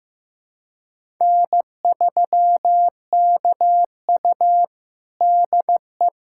Morse code, heard as a keyed tone, about 15 wpm, 700 Hz. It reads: N3KU DE
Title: USA Callsign DE Callsign - 4 Characters - 15wpm